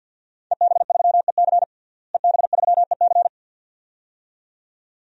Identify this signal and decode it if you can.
Morse code, heard as a keyed tone, about 47 wpm, 700 Hz. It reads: EB3EC EB3EC
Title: DXCC-100 Call Signs - 5 Characters - Repeated 2x - 50wpm